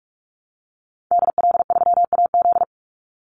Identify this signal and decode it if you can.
Morse code, heard as a keyed tone, about 44 wpm, 700 Hz. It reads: DL3AZ
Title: Call Signs Repeated 3x - 45wpm